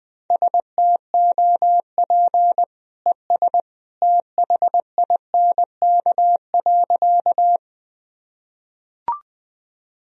Morse code, heard as a keyed tone, about 20 wpm, 700 Hz. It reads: STOP ES THINK.
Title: Sentences from Top 300 Words and Abbreviations 20wpm